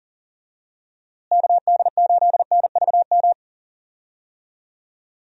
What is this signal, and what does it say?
Morse code, heard as a keyed tone, about 40 wpm, 700 Hz. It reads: KD8NVM
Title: USA Call Signs - 6 Characters - Repeated 2x - 40wpm